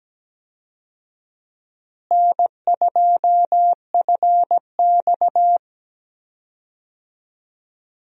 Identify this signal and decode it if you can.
Morse code, heard as a keyed tone, about 17 wpm, 700 Hz. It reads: N2FX
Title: USA Call Signs - 4 Characters - Repeated 2x - 17wpm